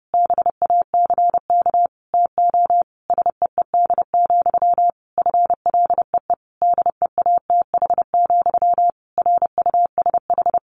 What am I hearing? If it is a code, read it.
Morse code, heard as a keyed tone, about 30 wpm, 700 Hz. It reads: BACK TO SEED, FLEE DEATH, RUSH
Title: A Sound Of Thunder 30wpm